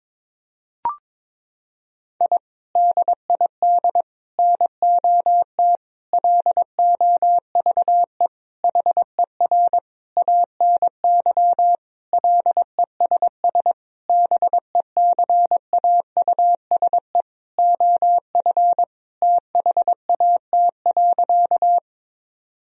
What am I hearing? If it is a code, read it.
Morse code, heard as a keyed tone, about 22 wpm, 700 Hz. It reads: I DID NOT LOVE HER ANY LESS BECAUSE OF THAT.
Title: Sentences from Top 1000 Words - Review Part 2 - 22wpm